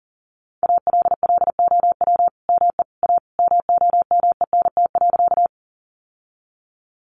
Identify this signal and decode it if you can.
Morse code, heard as a keyed tone, about 40 wpm, 700 Hz. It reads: ALLOW ME A MOMENT.
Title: Sentences from Top 1000 Words 40wpm